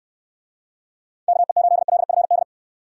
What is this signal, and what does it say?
Morse code, heard as a keyed tone, about 64 wpm, 700 Hz. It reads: KE8LFD